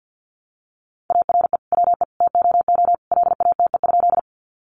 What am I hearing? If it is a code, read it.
Morse code, heard as a keyed tone, about 50 wpm, 700 Hz. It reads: ARE WE TOO LATE?
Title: Sentences from Top 300 Words 50wpm